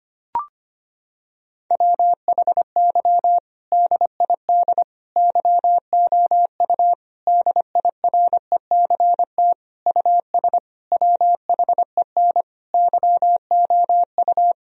Morse code, heard as a keyed tone, about 25 wpm, 700 Hz. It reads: WHY DID YOU DIRECT US WHEN YOU
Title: Sentences from Top 1000 Words - Review Part 2 - 25wpm